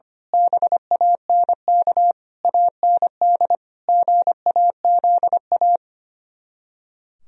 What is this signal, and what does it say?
Morse code, heard as a keyed tone, about 24 wpm, 700 Hz. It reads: BANK AND GAZA
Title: Speed-Racing - Sets of 5 Words 25wpm